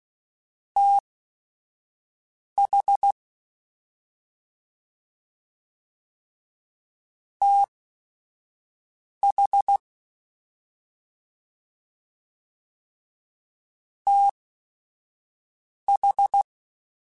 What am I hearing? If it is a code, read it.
Morse code, heard as a keyed tone, about 16 wpm, spaced as for 5 wpm, 800 Hz. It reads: TH TH TH
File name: Short Runs / Short Runs-mp3s / ShortContrasts @5